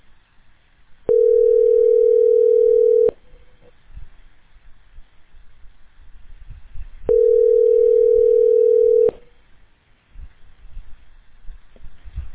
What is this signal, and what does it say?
Ringback tone (North America), 440 + 480 Hz.